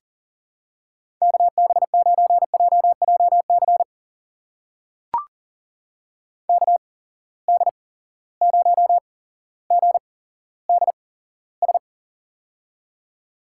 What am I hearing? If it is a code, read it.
Morse code, heard as a keyed tone, about 38 wpm, spaced as for 17 wpm, 700 Hz. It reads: KB9JJC KD0GDS
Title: USA Call Signs - 8x Character Spacing - 40wpm